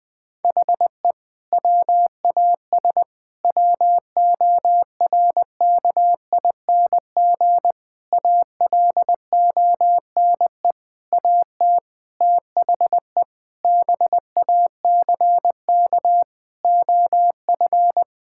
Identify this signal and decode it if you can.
Morse code, heard as a keyed tone, about 20 wpm, 700 Hz. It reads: HE WAS WORKING ALONE AT THE BACK OF
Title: Sentences from Top 1000 Words - Review Part 5 - 20wpm